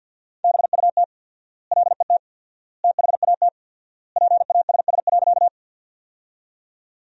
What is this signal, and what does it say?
Morse code, heard as a keyed tone, about 47 wpm, 700 Hz. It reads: BUT LET THAT PASS.
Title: 4x Word-Spacing - Sentences from Top 400 Words - 50wpm